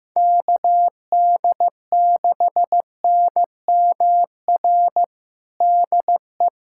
Morse code, heard as a keyed tone, about 15 wpm, 700 Hz. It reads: KD6NMR DE
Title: USA Callsign DE Callsign 15wpm